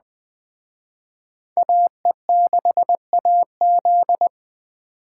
Morse code, heard as a keyed tone, about 20 wpm, 700 Hz. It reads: AE6AZ
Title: Speed Racing - USA 2x2 Call Signs 17wpm